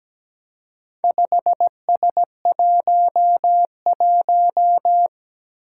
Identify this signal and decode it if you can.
Morse code, heard as a keyed tone, about 17 wpm, 700 Hz. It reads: 5S11
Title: Four Challenging Letter-Number Combinations 17wpm